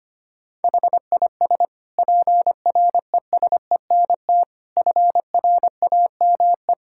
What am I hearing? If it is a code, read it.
Morse code, heard as a keyed tone, about 25 wpm, 700 Hz. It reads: HIS PRESENT FRAME
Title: Sentences from Top 1000 Words - Review Part 6 - 25wpm